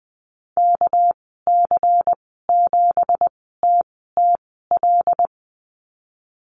Morse code, heard as a keyed tone, about 20 wpm, spaced as for 15 wpm, 700 Hz. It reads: KC7TTL
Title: 219 Lesson 35g - Number 0 - Call Signs 20wpm